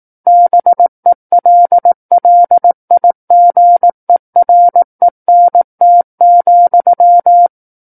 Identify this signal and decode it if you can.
Morse code, heard as a keyed tone, about 18 wpm, 700 Hz. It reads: BELLIGERENT,
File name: 18 WPM